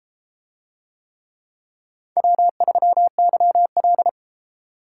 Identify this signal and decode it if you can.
Morse code, heard as a keyed tone, about 33 wpm, 700 Hz. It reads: W3YL